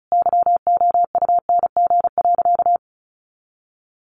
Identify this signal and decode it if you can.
Morse code, heard as a keyed tone, about 34 wpm, 700 Hz. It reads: YOUNG.
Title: Sentences from Top 400 Words 35wpm